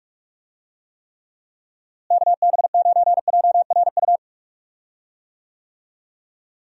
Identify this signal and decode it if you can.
Morse code, heard as a keyed tone, about 45 wpm, 700 Hz. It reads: KB9JRU
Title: USA Call Signs - 6 Characters - Repeated 2x - 45wpm